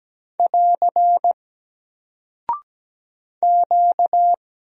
Morse code, heard as a keyed tone, about 17 wpm, 700 Hz. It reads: <AR> Q
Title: Single Character (Letter:Number:Punctuation:Prosign) Alt 17wpm